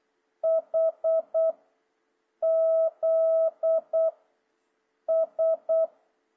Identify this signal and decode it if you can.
Morse code, heard as a keyed tone, about 8 wpm, 650 Hz. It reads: H Z S
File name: jh8cbh / 2026cw